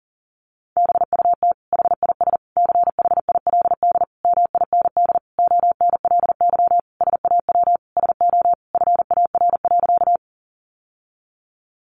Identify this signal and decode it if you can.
Morse code, heard as a keyed tone, about 40 wpm, 700 Hz. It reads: BUT HIS CHILD MIND ONLY SAW SO FAR.
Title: Sentences from Top 800 Words 40wpm